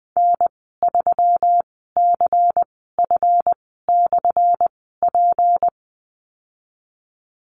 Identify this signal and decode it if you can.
Morse code, heard as a keyed tone, about 20 wpm, spaced as for 15 wpm, 700 Hz. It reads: N3CF/P
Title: 179 Lesson 29g - Slash - Call Signs 20wpm